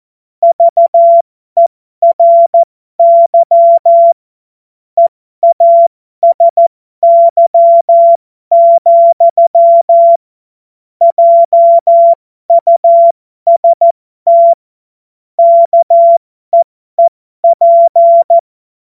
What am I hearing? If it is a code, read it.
Morse code, heard as a keyed tone, about 14 wpm, spaced as for 12 wpm, 680 Hz. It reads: VERY EASY, JUST KEEP